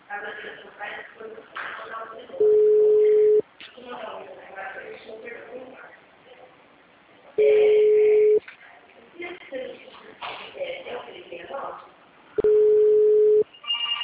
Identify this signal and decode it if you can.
Ringback tone (Europe, 425 Hz), with speech.
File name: out- -2017- - - .17050